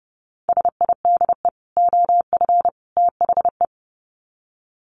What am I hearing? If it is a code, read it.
Morse code, heard as a keyed tone, about 30 wpm, 700 Hz. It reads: SIDE OF THE
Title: Sets of 3 Words 30wpm